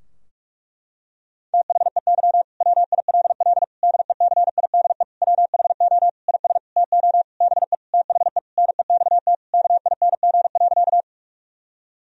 Morse code, heard as a keyed tone, about 46 wpm, 700 Hz. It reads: THEY WILL DECIDE WHO IS TO BE THE NEXT KING.
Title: Sentences from Top 1000 Words - Review Part 3 - 45wpm